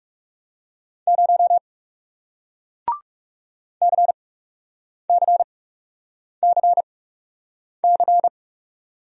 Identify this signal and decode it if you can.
Morse code, heard as a keyed tone, about 39 wpm, spaced as for 8 wpm, 700 Hz. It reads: 0 CCCC